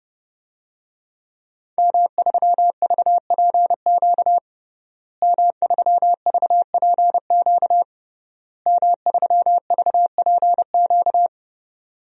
Morse code, heard as a keyed tone, about 30 wpm, 700 Hz. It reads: M3VPQ M3VPQ M3VPQ
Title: DXCC-100 Call Signs - 5 Characters - Repeated 3x - 30wpm